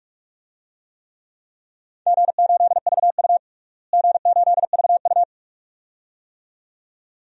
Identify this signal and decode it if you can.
Morse code, heard as a keyed tone, about 46 wpm, 700 Hz. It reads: G8VU G8VU
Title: DXCC-100 Call Signs - 4 Characters - Repeated 2x - 45wpm